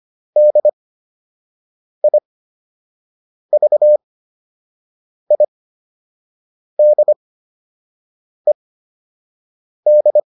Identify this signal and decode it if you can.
Morse code, heard as a keyed tone, about 25 wpm, spaced as for 6 wpm, 600 Hz. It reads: DIVIDED